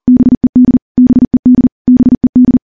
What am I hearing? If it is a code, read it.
Morse code, heard as a keyed tone, about 40 wpm, 250 Hz. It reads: BED BED BED
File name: morse / morse-training-40WPM